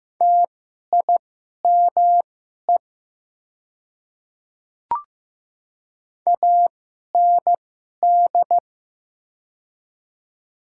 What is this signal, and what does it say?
Morse code, heard as a keyed tone, about 15 wpm, spaced as for 11 wpm, 690 Hz. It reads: TIME AND
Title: 2x Character Spacing - Top 100 Words - 15wpm